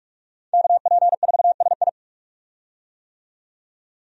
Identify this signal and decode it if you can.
Morse code, heard as a keyed tone, about 46 wpm, 700 Hz. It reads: KP4SI
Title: USA Call Signs - 5 Characters - 45wpm